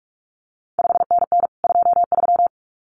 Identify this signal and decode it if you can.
Morse code, heard as a keyed tone, about 46 wpm, 700 Hz. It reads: 5NN 23
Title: CQ WW DX Contest Exchange Encore - 45wpm